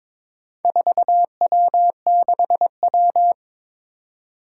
Morse code, heard as a keyed tone, about 22 wpm, 700 Hz. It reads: 4W6W
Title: Four Challenging Letter-Number Combinations 22wpm